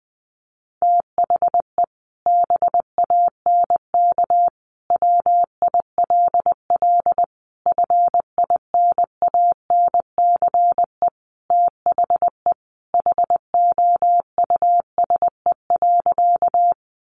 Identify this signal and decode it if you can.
Morse code, heard as a keyed tone, about 20 wpm, 700 Hz. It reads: THE BANK WILL FINANCE THE HOUSE.
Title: Sentences from Top 1500 Words 20wpm